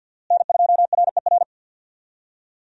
Morse code, heard as a keyed tone, about 50 wpm, 700 Hz. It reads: N2RER
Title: USA Call Signs - 5 Characters - Repeated 2x - 50wpm